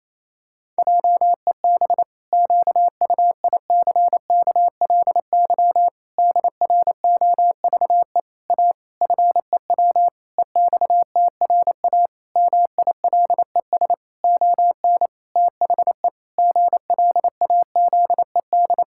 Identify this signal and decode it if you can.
Morse code, heard as a keyed tone, about 28 wpm, 700 Hz. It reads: JEB QUICKLY DROVE A FEW EXTRA MILES ON THE GLAZED
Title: Pangram Challenge v2 - 28wpm